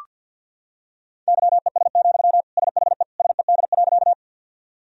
Morse code, heard as a keyed tone, about 46 wpm, 700 Hz. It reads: YES, SHE SED.
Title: Sentences from Top 500 Words and Abbreviations 50wpm